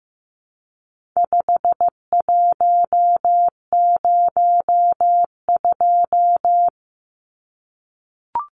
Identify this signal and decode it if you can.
Morse code, heard as a keyed tone, about 15 wpm, 700 Hz. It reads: 5102
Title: Numbers - 4-Digit 15wpm